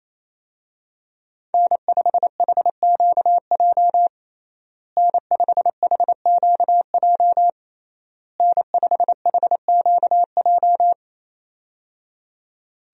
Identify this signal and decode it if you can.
Morse code, heard as a keyed tone, about 28 wpm, 700 Hz. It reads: N5HQJ N5HQJ N5HQJ
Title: USA Call Signs - 5 Characters - Repeated 3x - 28wpm